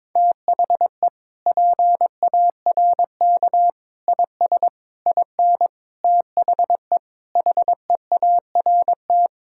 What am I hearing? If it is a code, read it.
Morse code, heard as a keyed tone, about 21 wpm, 700 Hz. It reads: THE PARK IS IN THE HEART
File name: practice / 22wpm / Sentences with 4-Word Phrases - 22wpm